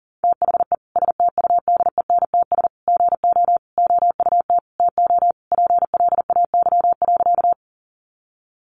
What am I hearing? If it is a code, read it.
Morse code, heard as a keyed tone, about 40 wpm, 700 Hz. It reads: THE STUDENTS GO OUT TO PLAY.
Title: Easy Phrases 40wpm